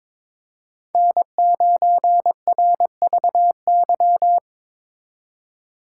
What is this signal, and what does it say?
Morse code, heard as a keyed tone, about 26 wpm, 700 Hz. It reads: N9RVY
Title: Speed Racing - USA 1x3 Call Signs 22wpm